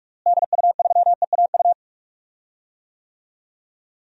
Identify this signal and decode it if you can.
Morse code, heard as a keyed tone, about 45 wpm, 700 Hz. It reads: DU3EAU
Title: DXCC-100 Call Signs - 6 Characters - Repeated 2x - 45wpm